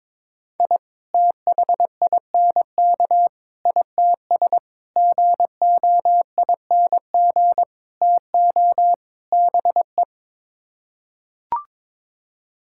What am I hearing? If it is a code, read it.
Morse code, heard as a keyed tone, about 22 wpm, 700 Hz. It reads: I THINK ITS GOING TO BE
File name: practice / 22wpm / Sets of 6 Words - 22wpm